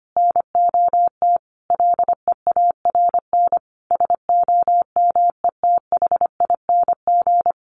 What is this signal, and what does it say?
Morse code, heard as a keyed tone, about 25 wpm, 700 Hz. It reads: NOT LEARN SOMETHING
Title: Sentences from Top 1400 Words 25wpm